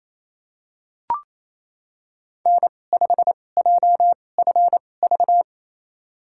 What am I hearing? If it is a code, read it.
Morse code, heard as a keyed tone, about 28 wpm, spaced as for 20 wpm, 700 Hz. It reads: N5JFV
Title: USA Call Signs - 5 Characters - 2x Character Spacing - 28wpm